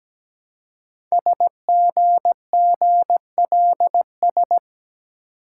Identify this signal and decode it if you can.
Morse code, heard as a keyed tone, about 17 wpm, 700 Hz. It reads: SGGLS
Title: Code Groups - 5 Letters - 17wpm